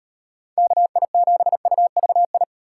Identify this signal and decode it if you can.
Morse code, heard as a keyed tone, about 36 wpm, 700 Hz. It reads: KI7UVI